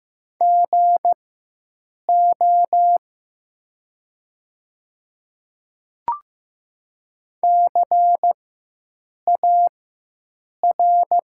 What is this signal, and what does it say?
Morse code, heard as a keyed tone, about 15 wpm, spaced as for 7 wpm, 700 Hz. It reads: GO CAR
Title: 4x Character Spacing - Top 300 Words - 15wpm